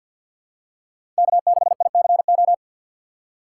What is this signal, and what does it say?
Morse code, heard as a keyed tone, about 52 wpm, 700 Hz. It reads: K6ICO
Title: USA Call Signs - 5 Characters - 6x Character Spacing - 50wpm